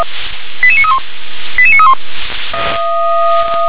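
Five-tone selective call (ZVEI), heard as quick runs of five tones, tones of about 70 ms. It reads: 88931 88931
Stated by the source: ZVEI five-tone selective call 88981 received by SDR.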